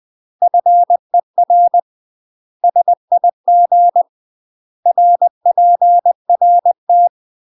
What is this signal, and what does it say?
Morse code, heard as a keyed tone, about 20 wpm, 700 Hz. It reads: FER SIG RPRT
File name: UK MASTER QSOs TRAINING / 20WPM / UK MASTER QSOs 0052